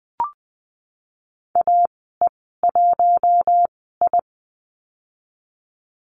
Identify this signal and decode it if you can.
Morse code, heard as a keyed tone, about 20 wpm, spaced as for 15 wpm, 700 Hz. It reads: AE1I
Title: 042 Lesson 8g - Number 1 - Call Signs 20wpm